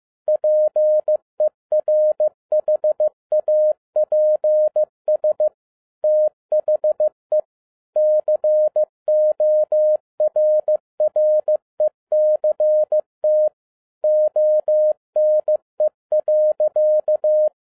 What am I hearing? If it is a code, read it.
Morse code, heard as a keyed tone, about 15 wpm, 610 Hz. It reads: PERHAPS THE CORRECT ONE.
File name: 2011-07-mp3 / n57 BACK FROM A BIG ACCIDENT 15wpm